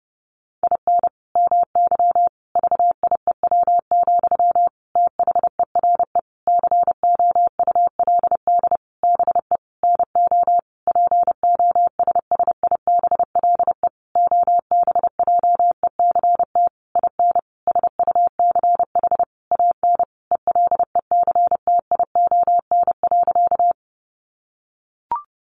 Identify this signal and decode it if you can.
Morse code, heard as a keyed tone, about 30 wpm, 700 Hz. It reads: IN MY VIEW, THERE COULD BE NO POSSIBLE OBJECT IN SUCH AN ELECTION.
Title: Sentences from Top 1500 Words 30wpm